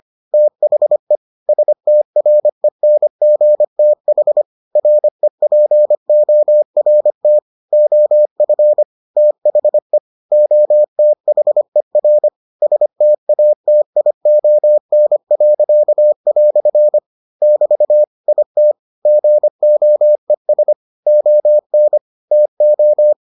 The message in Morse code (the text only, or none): THE STRENGTH REPORT OF THE OTHER STATION." <BT> IT GOES ON TO